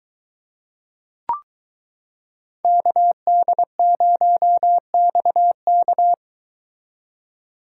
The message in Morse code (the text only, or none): KD0XK